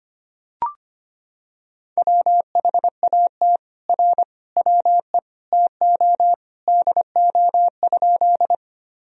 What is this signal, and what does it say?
Morse code, heard as a keyed tone, about 25 wpm, 700 Hz. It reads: WHAT R WE TO DO?
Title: Sentences from Top 100 Words and Abbreviations 25wpm